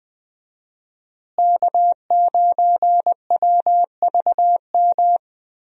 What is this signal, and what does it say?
Morse code, heard as a keyed tone, about 20 wpm, 700 Hz. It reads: K9WVM
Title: USA Call Signs - 5 Characters - Repeated 3x - 20wpm